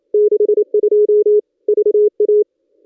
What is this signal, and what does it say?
Morse code, heard as a keyed tone, about 27 wpm, 420 Hz. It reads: 62 VA
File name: contests / ss-cw-k5tr-op-n5rz -1200